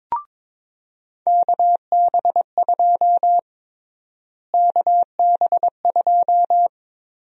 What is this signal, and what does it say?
Morse code, heard as a keyed tone, about 22 wpm, 700 Hz. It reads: KB2 KB2
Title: USA Call Sign Prefixes - Repeated 2x - 22wpm